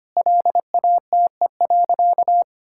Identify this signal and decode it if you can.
Morse code, heard as a keyed tone, about 25 wpm, 700 Hz. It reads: LATE.
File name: practice / 25wpm / Sentences with 3-Word Phrases - 25wpm